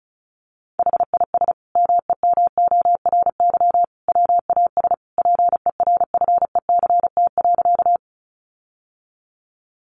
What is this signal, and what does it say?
Morse code, heard as a keyed tone, about 34 wpm, 700 Hz. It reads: HIS MEMORY WAS PERFECT.
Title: Sentences from Top 1300 Words 35wpm